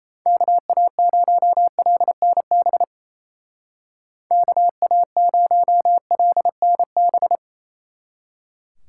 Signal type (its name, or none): Morse code